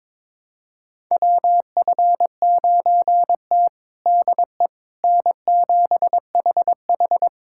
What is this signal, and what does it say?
Morse code, heard as a keyed tone, about 22 wpm, 700 Hz. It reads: WF9T DE N7HH
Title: USA Callsign DE Callsign - 4 Characters - 22wpm